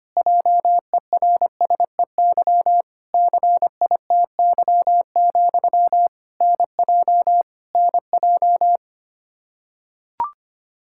Morse code, heard as a keyed tone, about 25 wpm, 700 Hz. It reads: JERSEY CITY, NJ NJ
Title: Practice QSO - Exchange 1 - Beginning - 25wpm